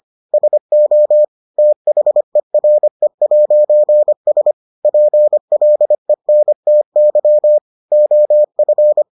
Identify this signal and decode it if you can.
Morse code, heard as a keyed tone, about 25 wpm, 600 Hz. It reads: SO THERE'S PLENTY OF